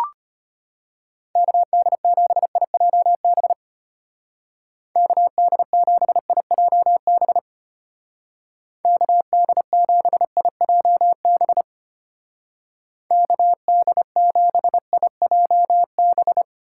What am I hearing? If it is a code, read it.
Morse code, heard as a keyed tone, about 31 wpm, 700 Hz. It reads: KD7IJB KD7IJB KD7IJB KD7IJB
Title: Speed Racing - USA 2x3 Call Signs 25wpm